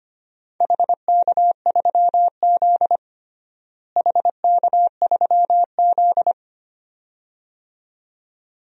Morse code, heard as a keyed tone, about 25 wpm, 700 Hz. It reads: HK3Z HK3Z
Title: DXCC-100 Call Signs - 4 Characters - Repeated 2x - 25wpm